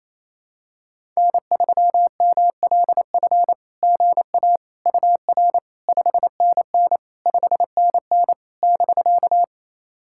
Morse code, heard as a keyed tone, about 28 wpm, 700 Hz. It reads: N3MLF GA UR 5NN 5NN <BK>
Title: SOTA QSO Practice 28wpm